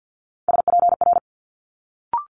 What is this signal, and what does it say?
Morse code, heard as a keyed tone, about 50 wpm, 700 Hz. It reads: SPR
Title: Multi-letter Phonemes Quick Fire 50wpm